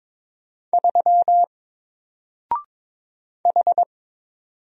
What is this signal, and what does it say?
Morse code, heard as a keyed tone, about 22 wpm, 700 Hz. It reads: 3 H
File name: Single-Letter-Number-22wpm